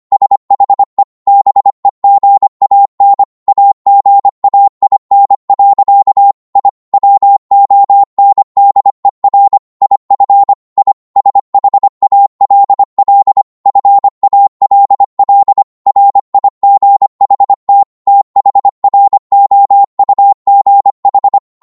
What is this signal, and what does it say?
Morse code, heard as a keyed tone, about 25 wpm, 800 Hz. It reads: SHE BEGAN AGAIN. I WONDER IF I SHALL FALL RIGHT THROUGH